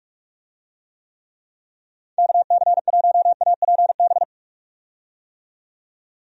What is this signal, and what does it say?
Morse code, heard as a keyed tone, about 47 wpm, 700 Hz. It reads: KC1APB
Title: USA Call Signs - 6 Characters - 45wpm